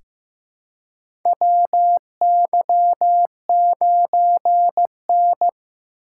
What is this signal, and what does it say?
Morse code, heard as a keyed tone, about 15 wpm, 700 Hz. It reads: WY9N
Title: USA Call Signs - 4 Characters - 2x Character Spacing - 15wpm